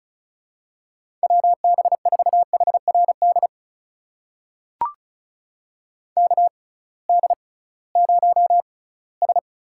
Morse code, heard as a keyed tone, about 35 wpm, spaced as for 20 wpm, 700 Hz. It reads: WB4HRD KD0S
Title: USA Call Signs - 6 Characters - 6x Character Spacing - 35wpm